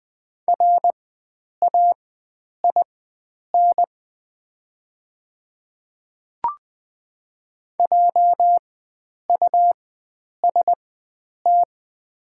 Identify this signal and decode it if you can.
Morse code, heard as a keyed tone, about 20 wpm, spaced as for 9 wpm, 700 Hz. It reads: RAIN JUST
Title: 4x Character Spacing - Top 300 Words - 20wpm